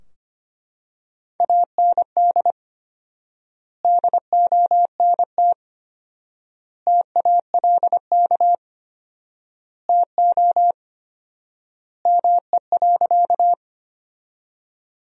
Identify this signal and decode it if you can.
Morse code, heard as a keyed tone, about 25 wpm, 700 Hz. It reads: AND DONT TALK TO ME.
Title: 4x Word-Spacing - Sentences from Top 400 Words - 25wpm